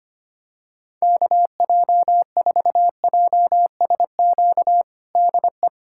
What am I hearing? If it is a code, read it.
Morse code, heard as a keyed tone, about 25 wpm, 700 Hz. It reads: KJ4JSQ DE
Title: USA Callsign DE Callsign 25wpm